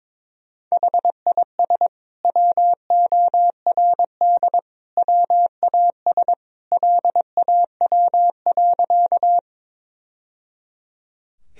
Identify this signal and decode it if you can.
Morse code, heard as a keyed tone, about 22 wpm, 700 Hz. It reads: HIS WORD WAS LAW.
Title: Sentences from Top 1000 Words - Review Part 3 - 22wpm